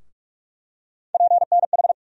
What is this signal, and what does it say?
Morse code, heard as a keyed tone, about 46 wpm, 700 Hz. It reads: PNH
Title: USA Call Sign Postfixes - 3 Letters 45wpm